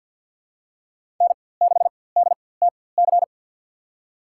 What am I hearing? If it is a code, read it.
Morse code, heard as a keyed tone, about 50 wpm, spaced as for 23 wpm, 700 Hz. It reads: N6DTC
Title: USA Call Signs - 5 Characters - 4x Character Spacing - 50wpm